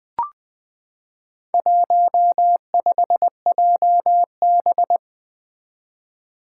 Morse code, heard as a keyed tone, about 20 wpm, 700 Hz. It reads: 15JB
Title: Four Challenging Letter-Number Combinations 20wpm